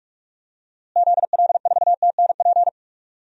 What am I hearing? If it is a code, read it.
Morse code, heard as a keyed tone, about 46 wpm, 700 Hz. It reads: ZL4TNP
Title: DXCC-100 Call Signs - 6 Characters - Repeated 3x - 45wpm